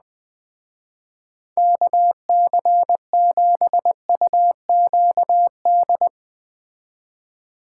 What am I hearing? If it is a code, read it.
Morse code, heard as a keyed tone, about 20 wpm, 700 Hz. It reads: KC7UQD